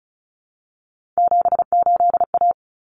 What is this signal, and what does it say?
Morse code, heard as a keyed tone, about 34 wpm, 700 Hz. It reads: 78A
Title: Common Cut Numbers - 3-Digit 35wpm